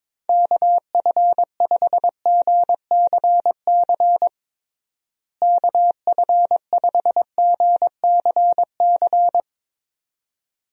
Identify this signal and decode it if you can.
Morse code, heard as a keyed tone, about 22 wpm, 700 Hz. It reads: KF5GCC KF5GCC
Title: USA Call Signs - 6 Characters - Repeated 2x - 22wpm